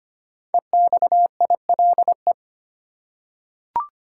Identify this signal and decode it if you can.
Morse code, heard as a keyed tone, about 25 wpm, 700 Hz. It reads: EXILE
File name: 5 Letter Words - First 3 Letters Given - 25wpm